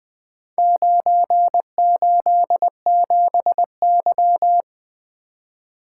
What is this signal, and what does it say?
Morse code, heard as a keyed tone, about 20 wpm, 700 Hz. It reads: 987Y
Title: Four Letter-Number Combinations - 20wpm